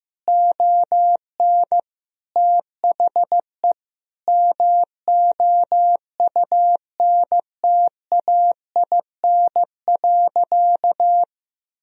Morse code, heard as a keyed tone, about 15 wpm, 700 Hz. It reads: ON THE MOUNTAIN.
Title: Sentences from Top 1000 Words - Review Part 1 - 15wpm